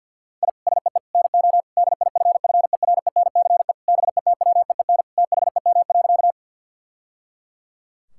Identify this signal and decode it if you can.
Morse code, heard as a keyed tone, about 50 wpm, 700 Hz. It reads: I SEE NO DIFFERENCE BETWEEN THEM.
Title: Sentences from Top 1000 Words - Review Part 5 - 50wpm